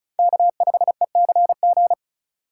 Morse code, heard as a keyed tone, about 36 wpm, 700 Hz. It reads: K5ECG
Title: USA Call Signs - 5 Characters - 2x Character Spacing - 35wpm